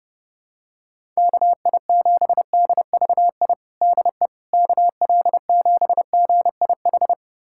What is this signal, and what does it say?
Morse code, heard as a keyed tone, about 30 wpm, 700 Hz. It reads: KI7DVI DE KL7GIH
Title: USA Callsign DE Callsign - 6 Characters - 30wpm